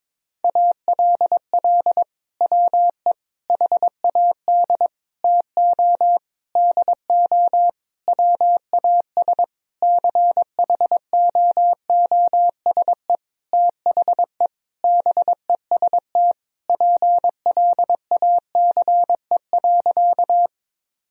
Morse code, heard as a keyed tone, about 22 wpm, 700 Hz. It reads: ALL WE HAD TO DO WAS CHOOSE THE BEST PLACE.
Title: Sentences from Top 1000 Words - Review Part 4 - 22wpm